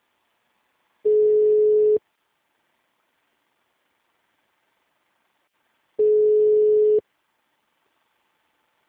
Ringback tone (Europe, 425 Hz).
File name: out- -2020- - - .77170